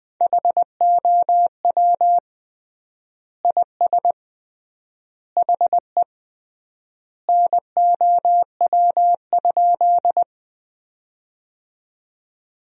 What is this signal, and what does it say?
Morse code, heard as a keyed tone, about 20 wpm, 700 Hz. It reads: HOW IS HE NOW?
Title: 3x Word-Spacing - Sentences from Top 100 Words - 20wpm